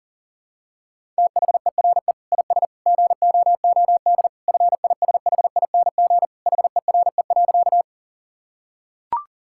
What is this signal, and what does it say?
Morse code, heard as a keyed tone, about 40 wpm, 700 Hz. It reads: THERE IS GOOD FISHING HERE.
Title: Sentences from Top 1000 Words - Review Part 5 - 40wpm